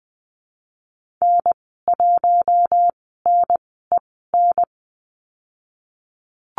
Morse code, heard as a keyed tone, about 20 wpm, spaced as for 15 wpm, 700 Hz. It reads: N1NEN
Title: 048 Lesson 9g - Number 4 - Call Signs 20wpm